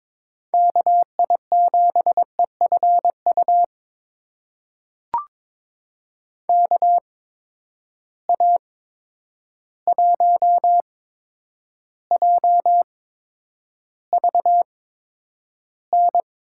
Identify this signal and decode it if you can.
Morse code, heard as a keyed tone, about 22 wpm, spaced as for 9 wpm, 700 Hz. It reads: KI7EFU KA1JVN
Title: USA Call Signs - 6 Characters - 8x Character Spacing - 22wpm